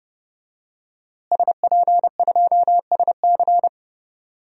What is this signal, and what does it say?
Morse code, heard as a keyed tone, about 30 wpm, 700 Hz. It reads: SP2SC
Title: DXCC-100 Call Signs - 5 Characters - Repeated 3x - 30wpm